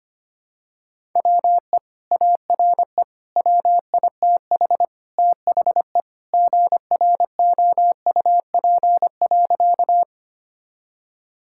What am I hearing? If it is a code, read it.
Morse code, heard as a keyed tone, about 25 wpm, 700 Hz. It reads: WE ARE WITH THE GROUP.
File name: practice / 25wpm / Sentences with 2-Word Phrases - 25wpm